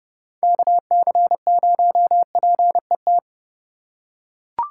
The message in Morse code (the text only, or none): KC0PET